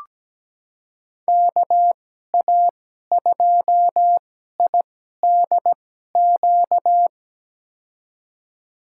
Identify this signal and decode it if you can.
Morse code, heard as a keyed tone, about 17 wpm, spaced as for 12 wpm, 700 Hz. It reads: KA2IDQ
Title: USA Call Signs - 6 Characters - 2x Character Spacing - 17wpm